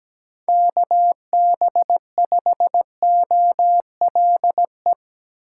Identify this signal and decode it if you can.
Morse code, heard as a keyed tone, about 19 wpm, 700 Hz. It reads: KB5OLE
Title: Speed Racing - USA 2x3 Call Signs 17wpm